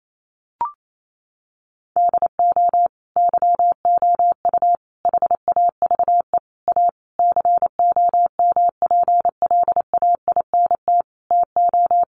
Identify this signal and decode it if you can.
Morse code, heard as a keyed tone, about 28 wpm, 700 Hz. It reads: DO YOU HAVE A COMPLAINT TO
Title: Sentences from Top 1400 Words 28wpm